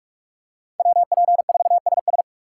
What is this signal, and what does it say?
Morse code, heard as a keyed tone, about 44 wpm, 700 Hz. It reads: WP4SS
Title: USA Call Signs - 5 Characters - 4x Character Spacing - 45wpm